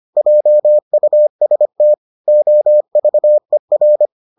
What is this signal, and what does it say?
Morse code, heard as a keyed tone, about 25 wpm, 600 Hz. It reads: JUST OVER